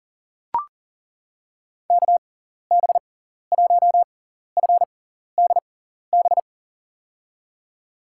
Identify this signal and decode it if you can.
Morse code, heard as a keyed tone, about 40 wpm, spaced as for 14 wpm, 700 Hz. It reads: KB1FDB
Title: USA Call Signs - 6 Characters - 6x Character Spacing - 40wpm